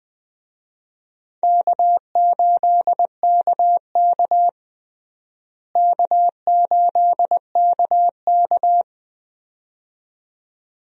Morse code, heard as a keyed tone, about 20 wpm, 700 Hz. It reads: K8KK K8KK
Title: USA Call Signs - 4 Characters - Repeated 2x - 20wpm